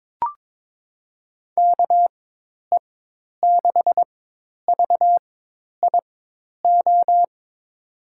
Morse code, heard as a keyed tone, about 22 wpm, spaced as for 10 wpm, 700 Hz. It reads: KE6VIO
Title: USA Call Signs - 4x Character Spacing - 22wpm